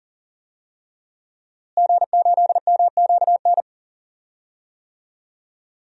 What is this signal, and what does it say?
Morse code, heard as a keyed tone, about 40 wpm, 700 Hz. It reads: G8MQN
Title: DXCC-100 Call Signs - 5 Characters - Repeated 3x - 40wpm